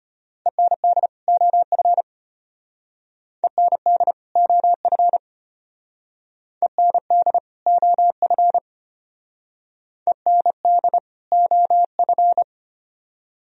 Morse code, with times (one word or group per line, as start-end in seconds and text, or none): 0.5-1.1 END
1.3-2.0 OF
3.4-4.1 END
4.4-5.2 OF
6.6-7.4 END
7.7-8.6 OF
10.1-11.0 END
11.3-12.4 OF